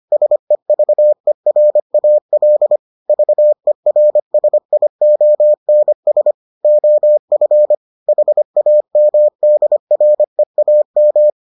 Morse code, heard as a keyed tone, about 25 wpm, 600 Hz. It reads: SEVERAL VERSIONS OF HAMDREAM